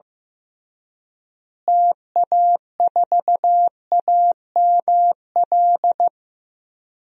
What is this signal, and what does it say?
Morse code, heard as a keyed tone, about 15 wpm, 700 Hz. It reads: TA4AML
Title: DXCC-100 Call Signs - 6 Characters - Repeated 3x - 15wpm